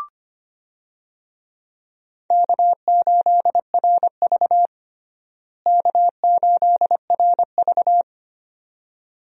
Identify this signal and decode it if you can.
Morse code, heard as a keyed tone, about 25 wpm, 700 Hz. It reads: K8RV K8RV
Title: USA Call Signs - 4 Characters - Repeated 2x - 25wpm